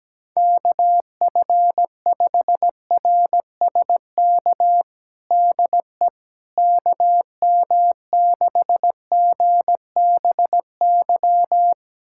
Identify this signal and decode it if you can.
Morse code, heard as a keyed tone, about 17 wpm, 700 Hz. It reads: KF5RSK DE KM6GBY
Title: USA Callsign DE Callsign - 6 Characters - 17wpm